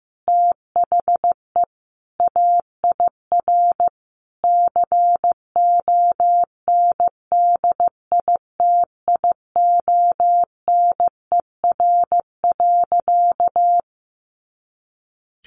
Morse code, heard as a keyed tone, about 15 wpm, 700 Hz. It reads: THE AIR CONDITIONER.
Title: Additional Easy Phrases 15wpm